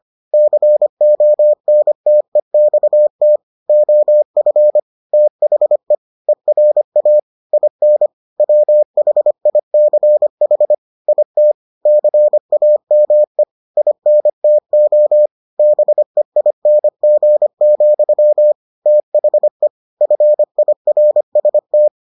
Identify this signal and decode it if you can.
Morse code, heard as a keyed tone, about 25 wpm, 600 Hz. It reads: CONTEXT OF THE ERA IN WHICH IT CAME INTO BEING, THE FIRST